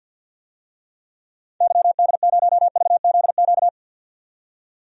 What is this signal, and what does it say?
Morse code, heard as a keyed tone, about 50 wpm, 700 Hz. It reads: YD0VZQ
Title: DXCC-100 Call Signs - 6 Characters - 50wpm